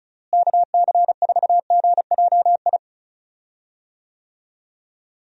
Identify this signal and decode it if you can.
Morse code, heard as a keyed tone, about 34 wpm, 700 Hz. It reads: KC4GJI
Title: USA Call Signs - 6 Characters - Repeated 2x - 35wpm